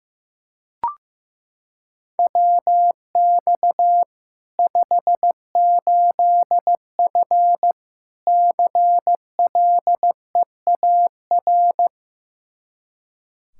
Morse code, heard as a keyed tone, about 15 wpm, 700 Hz. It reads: WX 58F CLEAR
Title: Practice QSO - Exchange 2 - Info - 15wpm